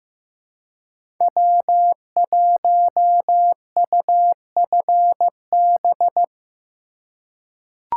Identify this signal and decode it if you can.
Morse code, heard as a keyed tone, about 16 wpm, 700 Hz. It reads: W1UFB
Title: USA Call Signs - 5 Characters - 2x Character Spacing - 15wpm